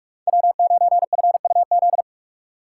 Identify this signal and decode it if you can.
Morse code, heard as a keyed tone, about 46 wpm, 700 Hz. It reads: W9FUZ
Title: USA Call Signs - 2x Character Spacing - 45wpm